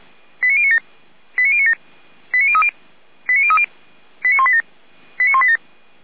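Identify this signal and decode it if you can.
Five-tone selective call (ZVEI), heard as quick runs of five tones, tones of about 70 ms. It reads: 89097 89097 89030 89030 89187 89187
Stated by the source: ZVEI five-tone selective call received by SDR.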